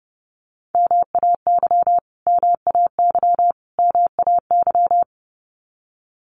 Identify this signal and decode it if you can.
Morse code, heard as a keyed tone, about 30 wpm, 700 Hz. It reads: MAY MAY MAY
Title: Top 500 Words Repeated 30wpm